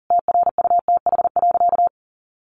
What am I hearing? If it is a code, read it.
Morse code, heard as a keyed tone, about 40 wpm, 700 Hz. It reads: TRUTH.
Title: Sentences from Top 1500 Words 40wpm